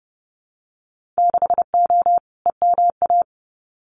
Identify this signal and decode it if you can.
Morse code, heard as a keyed tone, about 30 wpm, 700 Hz. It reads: 6O EMA
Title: Winter Field Day Exchanges Encore 30wpm